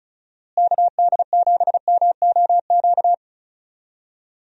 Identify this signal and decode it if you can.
Morse code, heard as a keyed tone, about 36 wpm, 700 Hz. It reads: KD7MOQ
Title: USA Call Signs - 6 Characters - Repeated 3x - 35wpm